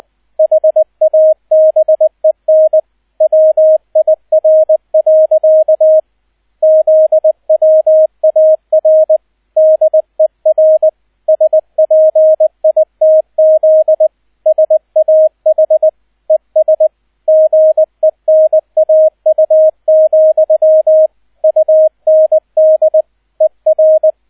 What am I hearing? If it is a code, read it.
Morse code, heard as a keyed tone, about 20 wpm, 630 Hz, with radio noise. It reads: HABEN WIR. ZWAR DER SPITZ SAH ES GENAU, UND ER
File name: 002 Zweiter Streich 20wpm